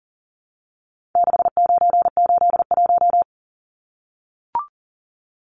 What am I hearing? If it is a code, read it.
Morse code, heard as a keyed tone, about 40 wpm, 700 Hz. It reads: 6981
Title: Numbers - 4-Digit 40wpm